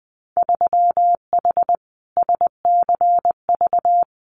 Morse code, heard as a keyed tone, about 20 wpm, 700 Hz. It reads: 3H SCV
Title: Winter Field Day Exchanges Encore 20wpm